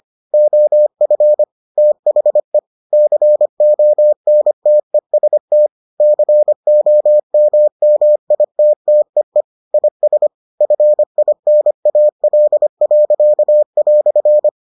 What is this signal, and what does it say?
Morse code, heard as a keyed tone, about 25 wpm, 600 Hz. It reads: OF THE CONTEST COMMITTEE IS FINAL."